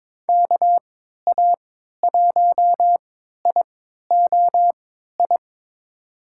Morse code, heard as a keyed tone, about 22 wpm, spaced as for 12 wpm, 700 Hz. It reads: KA1IOI
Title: USA Call Signs - 6 Characters - 3x Character Spacing - 22wpm